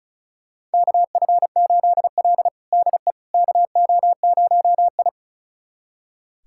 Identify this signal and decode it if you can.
Morse code, heard as a keyed tone, about 36 wpm, 700 Hz. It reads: KF8L DE KO0I
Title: USA Callsign DE Callsign - 4 Characters - 35wpm